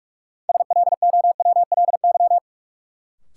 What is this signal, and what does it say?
Morse code, heard as a keyed tone, about 45 wpm, 700 Hz. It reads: SLOWLY
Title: Sentences from Top 1100 Words - New Word Drill Alt - 45wpm